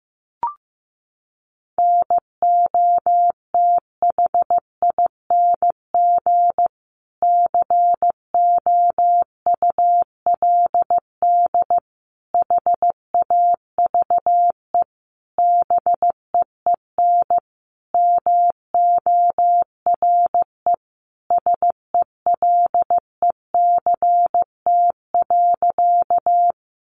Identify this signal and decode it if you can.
Morse code, heard as a keyed tone, about 15 wpm, 700 Hz. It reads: NOTHING COULD HAVE BEEN MORE SELECT.
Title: Sentences from Top 1300 Words 15wpm